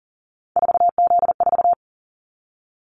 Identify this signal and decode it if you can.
Morse code, heard as a keyed tone, about 40 wpm, 700 Hz. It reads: 4Z4
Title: Callsign Prefixes for Top 50 DXCC - 40wpm